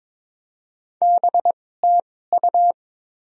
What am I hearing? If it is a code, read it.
Morse code, heard as a keyed tone, about 21 wpm, spaced as for 16 wpm, 690 Hz. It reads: BTU
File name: Top 20 QSO Elements - 2x Character Spacing - 22wpm